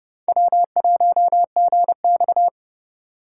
Morse code, heard as a keyed tone, about 31 wpm, 700 Hz. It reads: W1GX
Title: USA Call Signs - 4 Characters - 2x Character Spacing - 30wpm